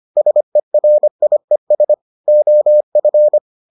Morse code, heard as a keyed tone, about 25 wpm, 600 Hz. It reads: SERIES OF